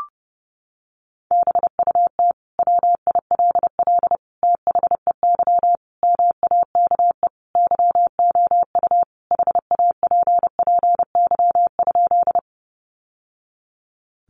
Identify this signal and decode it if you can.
Morse code, heard as a keyed tone, about 30 wpm, 700 Hz. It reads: BUT WILL THEY MAKE YOU HAPPY?
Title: Sentences from Top 1000 Words 30wpm